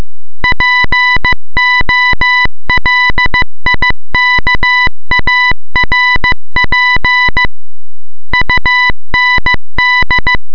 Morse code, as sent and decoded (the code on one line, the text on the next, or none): .--. --- .-.. .. -.- .- .-. .--. / ..- -. -..
POLIKARP UND